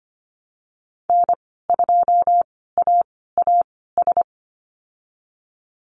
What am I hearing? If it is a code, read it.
Morse code, heard as a keyed tone, about 25 wpm, spaced as for 16 wpm, 700 Hz. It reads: N2AAS
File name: Lesson 15g - Number 5 - Call Signs 25wpm